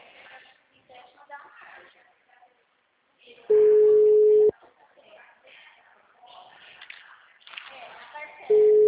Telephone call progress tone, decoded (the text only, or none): ringback tone